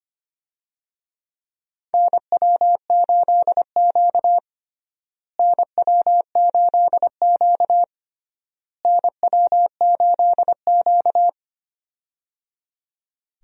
Morse code, heard as a keyed tone, about 25 wpm, 700 Hz. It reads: NW8Q NW8Q NW8Q
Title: USA Call Signs - 4 Characters - Repeated 3x - 25wpm